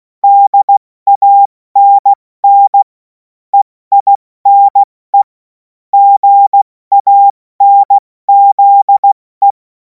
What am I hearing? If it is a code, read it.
Morse code, heard as a keyed tone, about 16 wpm, spaced as for 14 wpm, 800 Hz. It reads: DANN EINE GANZE
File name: Morsekurs / Witze / Witz 17. März 2026 70 F80